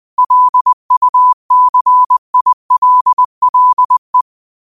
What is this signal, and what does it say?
Morse code, heard as a keyed tone, about 20 wpm, 1000 Hz. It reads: LUCILLE
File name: qotdfiles / 20wpm